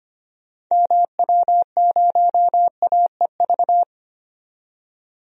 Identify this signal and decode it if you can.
Morse code, heard as a keyed tone, about 25 wpm, 700 Hz. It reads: MW0AEV
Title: DXCC-100 Call Signs - 6 Characters - 25wpm